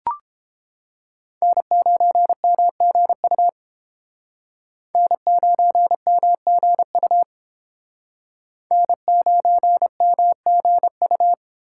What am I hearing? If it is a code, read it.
Morse code, heard as a keyed tone, about 32 wpm, 700 Hz. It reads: N9MGU N9MGU N9MGU